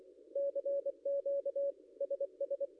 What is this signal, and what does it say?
Morse code, heard as a keyed tone, about 23 wpm, 560 Hz. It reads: CQ SS
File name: contests / ss-cw-k5tr-op-n5rz -0800